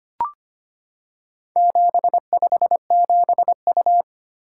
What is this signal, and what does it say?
Morse code, heard as a keyed tone, about 25 wpm, 700 Hz. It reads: 757U
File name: practice / 25wpm / Four Letter-Number Combinations - 25wpm